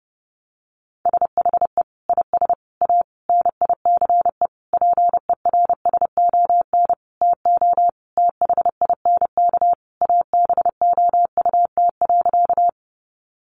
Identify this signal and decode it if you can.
Morse code, heard as a keyed tone, about 30 wpm, 700 Hz. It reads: SHE IS A NICE PERSON TO THINK ABOUT.
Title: Sentences from Top 1100 Words 30wpm